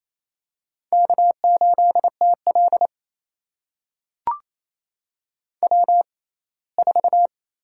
Morse code, heard as a keyed tone, about 28 wpm, spaced as for 19 wpm, 700 Hz. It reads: K8TL W4
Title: USA Call Signs - 4 Characters - 6x Character Spacing - 28wpm